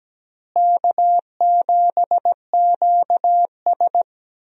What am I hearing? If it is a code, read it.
Morse code, heard as a keyed tone, about 17 wpm, 700 Hz. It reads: K7QS
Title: USA Call Signs - 4 Characters - Repeated 3x - 17wpm